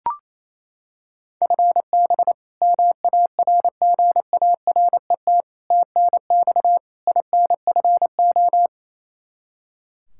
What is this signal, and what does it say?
Morse code, heard as a keyed tone, about 28 wpm, 700 Hz. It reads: FB MARGARET TNX INFO
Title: Practice QSO - Exchange 3 - More Info - 28wpm